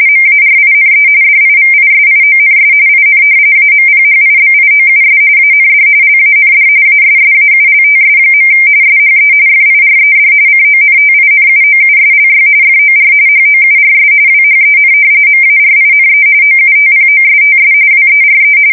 RTTY, about 45.45 baud. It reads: MARINE DIVISIONS IN AMPHIBIOUS AND LAND ASSAULTS. SHE FIRED CONTINUOUSLY FOR HOURS WHEN MARINE OUTPOSTS WERE
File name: fsk170 / battleshipnj